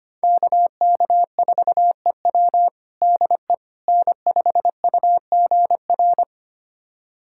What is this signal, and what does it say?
Morse code, heard as a keyed tone, about 25 wpm, 700 Hz. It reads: KK4EW DE N5UGR
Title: USA Callsign DE Callsign - 5 Characters - 25wpm